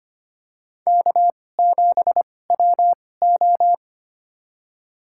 Morse code, heard as a keyed tone, about 25 wpm, spaced as for 18 wpm, 700 Hz. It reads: K7WO
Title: USA Call Signs - 4 Characters - 2x Character Spacing - 25wpm